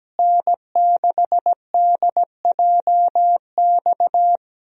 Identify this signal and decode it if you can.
Morse code, heard as a keyed tone, about 17 wpm, 700 Hz. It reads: N6DJX
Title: USA Call Signs - 5 Characters - Repeated 2x - 17wpm